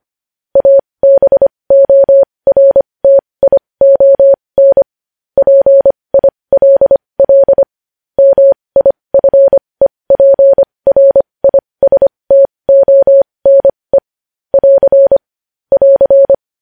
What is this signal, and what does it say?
Morse code, heard as a keyed tone, about 25 wpm, spaced as for 20 wpm, 550 Hz. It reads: ABORTION PILL MIFEPRISTONE <AR> <AR>
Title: News Headlines at 20